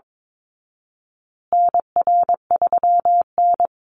Morse code, heard as a keyed tone, about 22 wpm, 700 Hz. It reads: NR3N
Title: Call Signs Repeated 3x - 22wpm